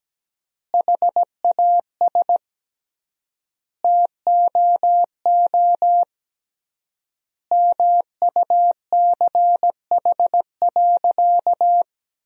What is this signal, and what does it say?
Morse code, heard as a keyed tone, about 17 wpm, 700 Hz. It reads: HAS TOO MUCH.
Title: 3x Word-Spacing - Sentences from Top 300 Words - 17wpm